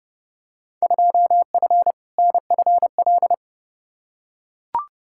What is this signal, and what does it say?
Morse code, heard as a keyed tone, about 32 wpm, 700 Hz. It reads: 2F NFL
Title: Field Day Exchanges - 30wpm